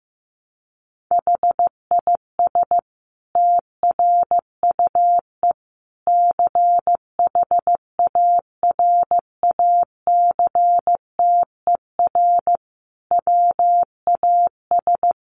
Morse code, heard as a keyed tone, about 15 wpm, 700 Hz. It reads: HIS TRUE CHARACTER WAS
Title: Sentences from Top 1200 Words 15wpm